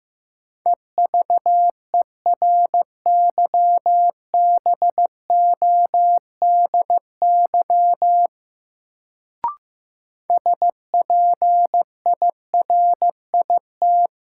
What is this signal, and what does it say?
Morse code, heard as a keyed tone, about 15 wpm, 700 Hz. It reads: EVERYBODY SPIRIT
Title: Sentences from Top 1400 Words - New Word Drill Alt - 15wpm